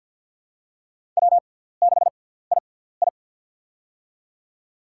Morse code, heard as a keyed tone, about 48 wpm, spaced as for 17 wpm, 700 Hz. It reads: W6II
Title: USA Call Signs - 4 Characters - 6x Character Spacing - 50wpm